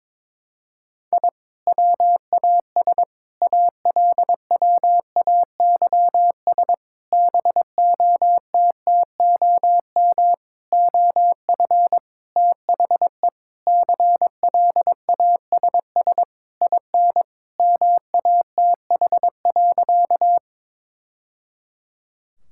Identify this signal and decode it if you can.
Morse code, heard as a keyed tone, about 22 wpm, 700 Hz. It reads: I WAS ALWAYS BOTTOM OF THE CLASS IN MATH.
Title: Sentences from Top 1000 Words - Review Part 6 - 22wpm